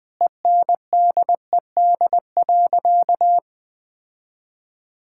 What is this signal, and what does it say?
Morse code, heard as a keyed tone, about 20 wpm, 700 Hz. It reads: ENDED.
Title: Sentences with 2-Word Phrases - 20wpm